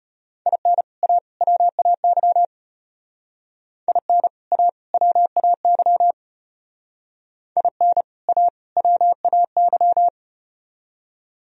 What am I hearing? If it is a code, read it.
Morse code, heard as a keyed tone, about 34 wpm, 700 Hz. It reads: IN A WAY IN A WAY IN A WAY